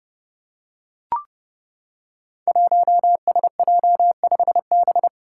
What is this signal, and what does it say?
Morse code, heard as a keyed tone, about 30 wpm, 700 Hz. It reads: 1SJ5B
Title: Five Challenging Letter-Number Combinations 30wpm